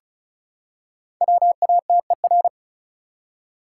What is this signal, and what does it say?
Morse code, heard as a keyed tone, about 34 wpm, 700 Hz. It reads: WATER
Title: Five Letter Words - 35wpm